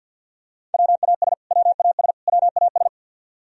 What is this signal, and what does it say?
Morse code, heard as a keyed tone, about 54 wpm, 700 Hz. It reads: WAS WAS WAS
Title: Top 100 Words Repeated 50wpm